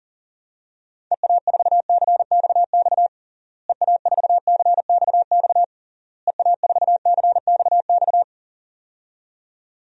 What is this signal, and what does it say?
Morse code, heard as a keyed tone, about 40 wpm, 700 Hz. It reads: EA4CXX EA4CXX EA4CXX
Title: DXCC-100 Call Signs - 6 Characters - Repeated 3x - 40wpm